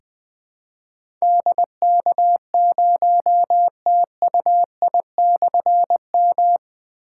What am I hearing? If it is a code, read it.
Morse code, heard as a keyed tone, about 20 wpm, 700 Hz. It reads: DK0TUI/M
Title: Difficult Call Signs 20wpm